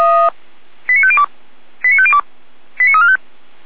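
Five-tone selective call (ZVEI), heard as quick runs of five tones, tones of about 70 ms. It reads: 89502 89502 89345
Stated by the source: ZVEI five-tone selective call 89130 received on an SDR by Feuerwehr Orlamünde.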